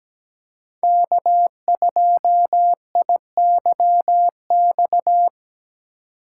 Morse code, heard as a keyed tone, about 20 wpm, 700 Hz. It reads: K2IYX
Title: Speed Racing - USA 1x3 Call Signs 17wpm